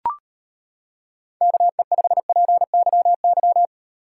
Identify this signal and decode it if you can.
Morse code, heard as a keyed tone, about 30 wpm, 700 Hz. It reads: KE5PYY
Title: Speed Racing - USA 2x3 Call Signs 25wpm